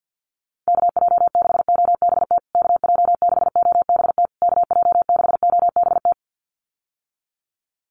Morse code, heard as a keyed tone, about 48 wpm, 700 Hz. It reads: KJ6OBT KJ6OBT KJ6OBT
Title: USA Call Signs Repeated 3x 50wpm